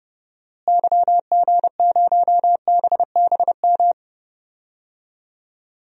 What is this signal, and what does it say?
Morse code, heard as a keyed tone, about 30 wpm, 700 Hz. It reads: YG0BBM
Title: DXCC-100 Call Signs - 6 Characters - Repeated 3x - 30wpm